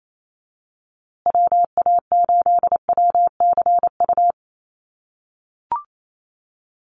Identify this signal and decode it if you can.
Morse code, heard as a keyed tone, about 27 wpm, 700 Hz. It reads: WA8WCU
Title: USA Call Signs Repeated 3x 28wpm